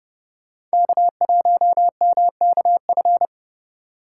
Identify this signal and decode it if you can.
Morse code, heard as a keyed tone, about 30 wpm, 700 Hz. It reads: K1MKF
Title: USA Call Signs - 5 Characters - 3x Character Spacing - 30wpm